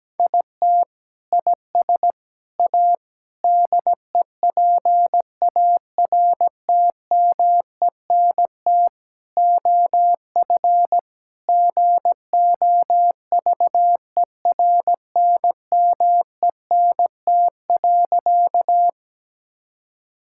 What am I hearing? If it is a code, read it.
Morse code, heard as a keyed tone, about 17 wpm, 700 Hz. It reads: IT IS A DEPARTMENT OF GOVERNMENT.
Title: Sentences from Top 1000 Words - Review Part 5 - 17wpm